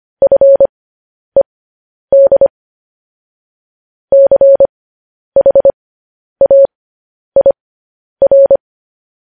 Morse code, heard as a keyed tone, about 25 wpm, spaced as for 10 wpm, 550 Hz. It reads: FED CHAIR
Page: News Headlines in Morse Code at 10 WPM